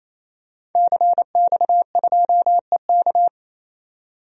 Morse code, heard as a keyed tone, about 28 wpm, 700 Hz. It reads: CX2EK
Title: DXCC-100 Call Signs - 5 Characters - 28wpm